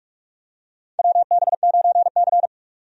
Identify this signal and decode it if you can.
Morse code, heard as a keyed tone, about 46 wpm, 700 Hz. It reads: WB9C
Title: USA Call Signs - 4 Characters - 45wpm